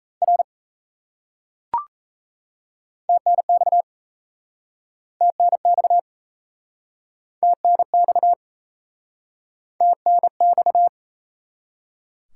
Morse code, heard as a keyed tone, about 36 wpm, 700 Hz. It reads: R TNX TNX TNX TNX